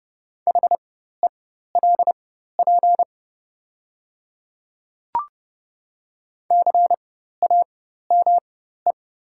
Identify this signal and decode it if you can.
Morse code, heard as a keyed tone, about 30 wpm, spaced as for 14 wpm, 700 Hz. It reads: HELP CAME
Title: 4x Character Spacing - Top 200 Words - 30wpm